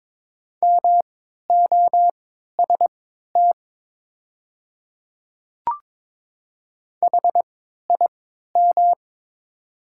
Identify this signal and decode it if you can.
Morse code, heard as a keyed tone, about 23 wpm, spaced as for 13 wpm, 700 Hz. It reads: MOST HIM
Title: 3x Character Spacing - Top 100 Words - 22wpm